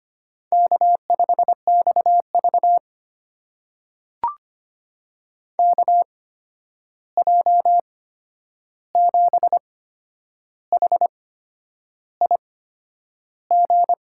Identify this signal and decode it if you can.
Morse code, heard as a keyed tone, about 25 wpm, spaced as for 9 wpm, 700 Hz. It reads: K5XV KJ7HIG
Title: USA Call Signs - 8x Character Spacing - 25wpm